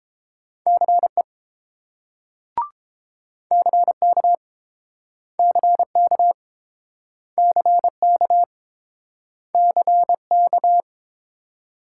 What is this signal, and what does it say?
Morse code, heard as a keyed tone, about 29 wpm, 700 Hz. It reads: CE CK CK CK CK